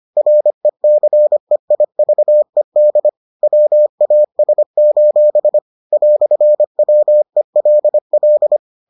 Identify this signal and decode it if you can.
Morse code, heard as a keyed tone, about 25 wpm, 600 Hz. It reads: RECEIVED WAS: "WELL